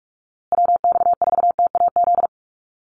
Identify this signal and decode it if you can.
Morse code, heard as a keyed tone, about 44 wpm, 700 Hz. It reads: WX4TAZ
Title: USA Call Signs Repeated 2x 45wpm